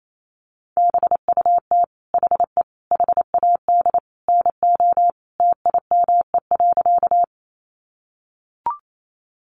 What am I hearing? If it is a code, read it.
Morse code, heard as a keyed tone, about 28 wpm, 700 Hz. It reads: BUT HE HAD NO TIME.
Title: Sentences from Top 100 Words 28wpm